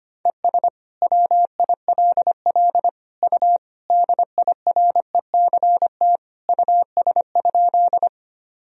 Morse code, heard as a keyed tone, about 25 wpm, 700 Hz. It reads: ES WILL U DIRECT US?
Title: Sentences from Top 400 Words and Abbreviations 25wpm